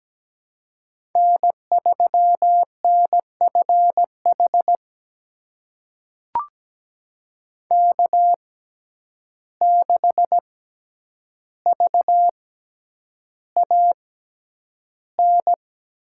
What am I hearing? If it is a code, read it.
Morse code, heard as a keyed tone, about 17 wpm, spaced as for 9 wpm, 700 Hz. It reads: N3NFH K6VAN
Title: USA Call Signs - 5 Characters - 6x Character Spacing - 17wpm